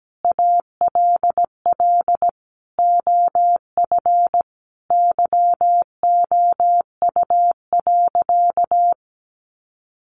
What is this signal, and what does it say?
Morse code, heard as a keyed tone, about 17 wpm, 700 Hz. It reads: ALL OF YOU.
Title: Sentences from Top 600 Words 17wpm